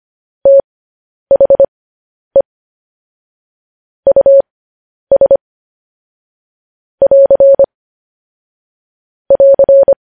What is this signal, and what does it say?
Morse code, heard as a keyed tone, about 25 wpm, spaced as for 10 wpm, 550 Hz. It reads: THE US <AR> <AR>
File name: podcast / News headlines-1x-25x10